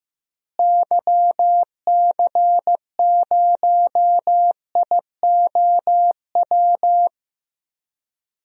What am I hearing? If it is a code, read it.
Morse code, heard as a keyed tone, about 15 wpm, 700 Hz. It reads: YC0IOW
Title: DXCC-100 Call Signs - 6 Characters - Repeated 3x - 15wpm